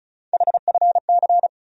Morse code, heard as a keyed tone, about 36 wpm, 700 Hz. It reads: HFC
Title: USA Call Sign Postfixes - 3 Letters 35wpm